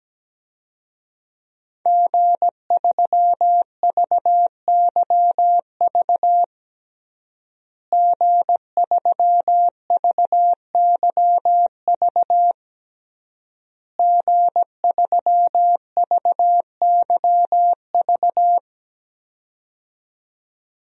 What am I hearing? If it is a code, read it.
Morse code, heard as a keyed tone, about 17 wpm, 700 Hz. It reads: G3VYV G3VYV G3VYV
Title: DXCC-100 Call Signs - 5 Characters - Repeated 3x - 17wpm